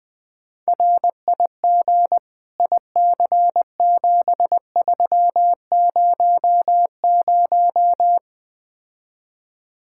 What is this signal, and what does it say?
Morse code, heard as a keyed tone, about 20 wpm, 700 Hz. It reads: RIG IC7300
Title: Speed-Racing Common QSO Phrases - 17wpm